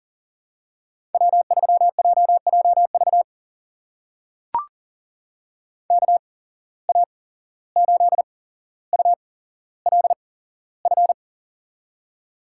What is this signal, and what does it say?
Morse code, heard as a keyed tone, about 42 wpm, spaced as for 16 wpm, 690 Hz. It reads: W3JJV KA8ULF
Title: USA Call Signs - 8x Character Spacing - 40wpm